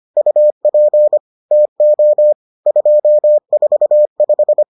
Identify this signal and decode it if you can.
Morse code, heard as a keyed tone, about 25 wpm, 600 Hz. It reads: UP TO 245